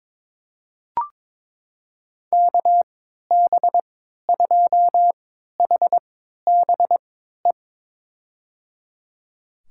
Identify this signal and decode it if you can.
Morse code, heard as a keyed tone, about 22 wpm, spaced as for 12 wpm, 700 Hz. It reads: KB2HBE
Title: USA Call Signs - 6 Characters - 3x Character Spacing - 22wpm